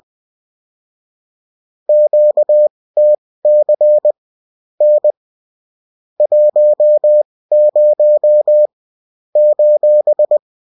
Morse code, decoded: QTC N 10 :